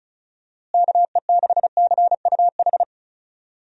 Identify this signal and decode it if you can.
Morse code, heard as a keyed tone, about 36 wpm, 700 Hz. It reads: KE6CUH
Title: USA Call Signs - 6 Characters - Repeated 3x - 35wpm